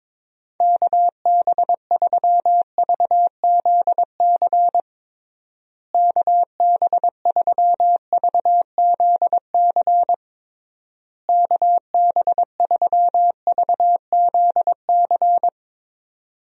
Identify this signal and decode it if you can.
Morse code, heard as a keyed tone, about 22 wpm, 700 Hz. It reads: KB3VZC KB3VZC KB3VZC
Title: USA Call Signs - 6 Characters - Repeated 3x - 22wpm